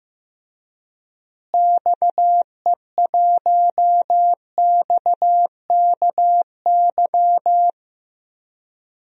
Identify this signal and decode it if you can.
Morse code, heard as a keyed tone, about 15 wpm, 700 Hz. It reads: XE1XKY
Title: DXCC-100 Call Signs - 6 Characters - Repeated 3x - 15wpm